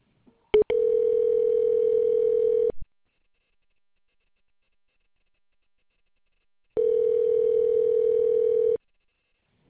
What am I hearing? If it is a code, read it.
Ringback tone (North America), 440 + 480 Hz.